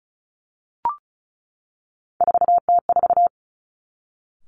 Morse code, heard as a keyed tone, about 36 wpm, 700 Hz. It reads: 4T4
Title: Three Challenging Letter-Number Combinations 35wpm